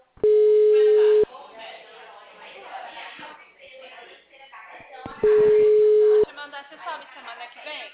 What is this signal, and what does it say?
Ringback tone (Europe, 425 Hz), with speech.
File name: out- -2022- - - .35